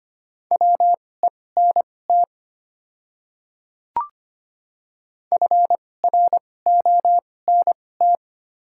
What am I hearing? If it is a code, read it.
Morse code, heard as a keyed tone, about 25 wpm, spaced as for 18 wpm, 700 Hz. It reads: WENT FRONT
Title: 2x Character Spacing - Top 500 Words - 25wpm